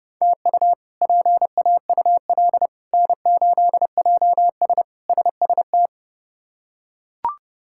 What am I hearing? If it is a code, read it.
Morse code, heard as a keyed tone, about 30 wpm, 700 Hz. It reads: TU PAUL N8JS SST
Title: K1USN SST QSO Practice - 30wpm